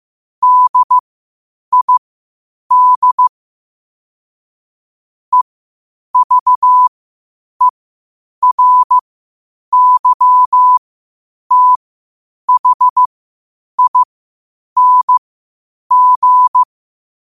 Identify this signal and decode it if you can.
Morse code, heard as a keyed tone, about 15 wpm, spaced as for 8 wpm, 1000 Hz. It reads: DID EVERYTHING
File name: qotdfiles / 8wpm